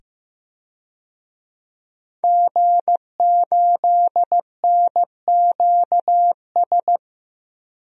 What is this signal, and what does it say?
Morse code, heard as a keyed tone, about 15 wpm, 700 Hz. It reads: G8NQS
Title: DXCC-100 Call Signs - 5 Characters - 15wpm